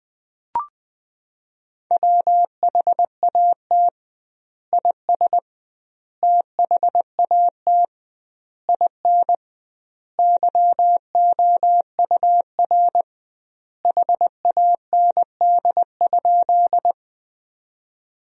Morse code, heard as a keyed tone, about 21 wpm, 700 Hz. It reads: WHAT IS THAT IN YOUR HAND?
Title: 2x Word-Spacing - Sentences from Top 200 Words - 20wpm